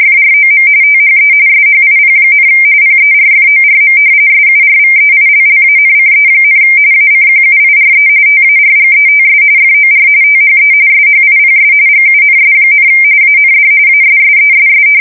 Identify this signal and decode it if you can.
RTTY, about 45.45 baud. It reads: FROM THE SHIP, PERFORM SURVEILLANCE, ACQUIRE TARGETS FOR THE SHIPS GUNS, AND ASSESS
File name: fsk170 / battleshipnj